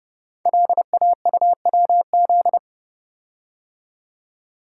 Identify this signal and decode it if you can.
Morse code, heard as a keyed tone, about 30 wpm, 700 Hz. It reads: LAUWZ
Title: Code Groups - 5 Letter:Numbers - 30wpm